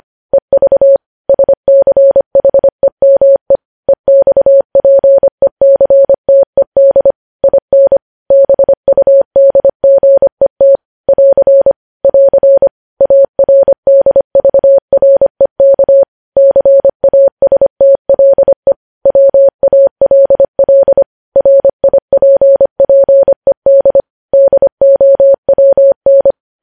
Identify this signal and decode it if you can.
Morse code, heard as a keyed tone, about 25 wpm, 550 Hz. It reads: EV SCHEME EXPECTED IN BUDGET <AR> <AR> ARDVREK CASTLE WALL RIPPED DOWN
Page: News Headlines in Morse Code at 25 WPM